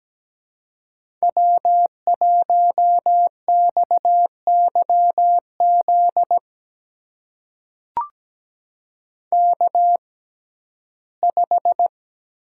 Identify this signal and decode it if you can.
Morse code, heard as a keyed tone, about 17 wpm, spaced as for 12 wpm, 700 Hz. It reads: W1XYZ K5
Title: USA Call Signs - 5 Characters - 6x Character Spacing - 17wpm